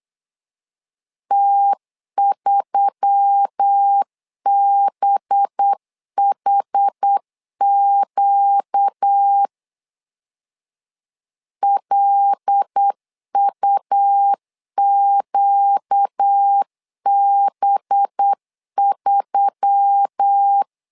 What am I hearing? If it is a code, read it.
Morse code, heard as a keyed tone, about 8 wpm, 800 Hz. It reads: T3BHQ LUQB3
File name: LEZIONI / 5 LBVC / 5 LB 8wpm 2816